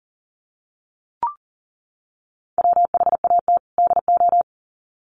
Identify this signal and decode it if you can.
Morse code, heard as a keyed tone, about 40 wpm, 700 Hz. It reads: WHAT DO
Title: Sets of 2 Words 40wpm